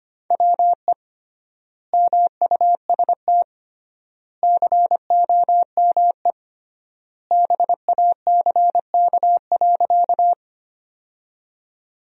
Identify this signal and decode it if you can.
Morse code, heard as a keyed tone, about 25 wpm, 700 Hz. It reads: WE MUST COME BACK.
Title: 3x Word-Spacing - Sentences from Top 200 Words - 25wpm